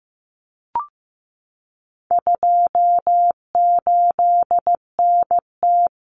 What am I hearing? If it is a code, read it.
Morse code, heard as a keyed tone, about 15 wpm, 700 Hz. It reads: 28NT
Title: Common Cut Numbers - 4-Digit 15wpm